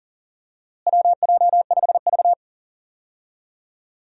Morse code, heard as a keyed tone, about 40 wpm, 700 Hz. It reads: WJ5V
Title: USA Call Signs - 4 Characters - Repeated 2x - 40wpm